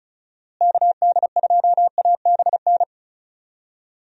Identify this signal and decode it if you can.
Morse code, heard as a keyed tone, about 36 wpm, 700 Hz. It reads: KD2ABN
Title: USA Call Signs - 6 Characters - Repeated 2x - 35wpm